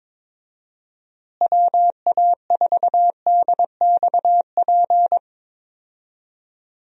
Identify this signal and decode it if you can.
Morse code, heard as a keyed tone, about 22 wpm, 700 Hz. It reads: WA4DXP
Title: USA Call Signs - 6 Characters - Repeated 3x - 22wpm